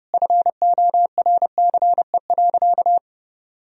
Morse code, heard as a keyed tone, about 30 wpm, 700 Hz. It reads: FORCE.
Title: Sentences from Top 1000 Words - Review Part 3 - 30wpm